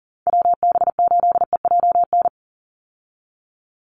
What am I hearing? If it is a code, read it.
Morse code, heard as a keyed tone, about 40 wpm, 700 Hz. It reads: WB8EJN
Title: Call Signs Repeated 3x - 40wpm